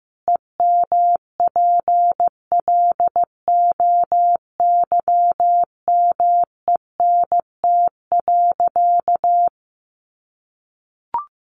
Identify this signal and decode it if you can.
Morse code, heard as a keyed tone, about 15 wpm, 700 Hz. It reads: EMPLOYMENT.
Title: Sentences from Top 1500 Words 15wpm